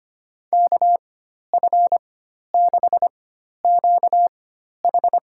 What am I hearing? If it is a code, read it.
Morse code, heard as a keyed tone, about 25 wpm, spaced as for 12 wpm, 700 Hz. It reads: KF6QH
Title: USA Call Signs - 4x Character Spacing - 25wpm